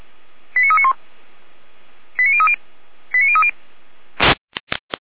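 Five-tone selective call (ZVEI), heard as quick runs of five tones, tones of about 70 ms. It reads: 89391 89030 89030
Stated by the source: Received by SDR as ZVEI five-tone selective call.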